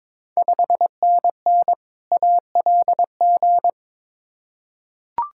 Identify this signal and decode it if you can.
Morse code, heard as a keyed tone, about 22 wpm, 700 Hz. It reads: 5NN ALG
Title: Ontario QSO Party Exchange 22wpm